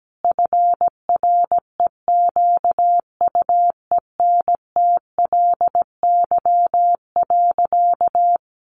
Morse code, heard as a keyed tone, about 17 wpm, 700 Hz. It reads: FREQUENTLY.
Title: Sentences from Top 1300 Words 17wpm